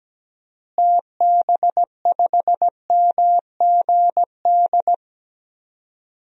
Morse code, heard as a keyed tone, about 17 wpm, 700 Hz. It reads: TB5MGD
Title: DXCC-100 Call Signs - 6 Characters - Repeated 2x - 17wpm